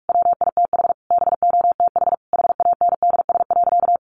Morse code, heard as a keyed tone, about 44 wpm, 700 Hz. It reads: WITH BOTH HANDS.
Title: Easy Phrases 45wpm